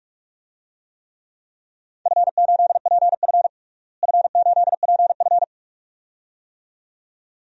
Morse code, heard as a keyed tone, about 45 wpm, 700 Hz. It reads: F8PF F8PF
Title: DXCC-100 Call Signs - 4 Characters - Repeated 2x - 45wpm